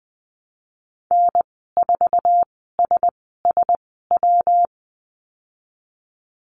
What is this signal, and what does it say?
Morse code, heard as a keyed tone, about 20 wpm, spaced as for 14 wpm, 700 Hz. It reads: N4SSW
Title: 122 Lesson 20g - Number 3 - Call Signs 20wpm